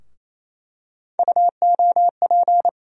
Morse code, heard as a keyed tone, about 27 wpm, 700 Hz. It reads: UOP
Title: USA Call Sign Postfixes - 3 Letters 28wpm